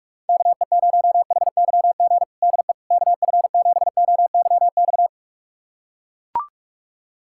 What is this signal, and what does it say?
Morse code, heard as a keyed tone, about 41 wpm, 700 Hz. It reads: KE0HYG DE KF7OYX
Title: USA Callsign DE Callsign - 6 Characters - 45wpm